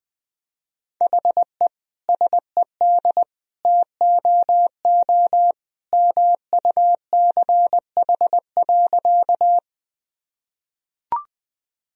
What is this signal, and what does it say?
Morse code, heard as a keyed tone, about 20 wpm, 700 Hz. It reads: HE SED TOO MUCH.
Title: Sentences from Top 200 Words and Abbreviations 20wpm